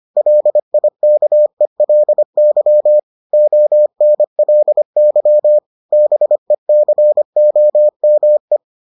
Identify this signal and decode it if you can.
Morse code, heard as a keyed tone, about 25 wpm, 600 Hz. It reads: LIKELY ONLY BECOME